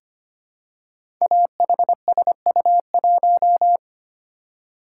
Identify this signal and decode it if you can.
Morse code, heard as a keyed tone, about 25 wpm, 700 Hz. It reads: AHSU1
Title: Code Groups - 5 Letter:Numbers - 25wpm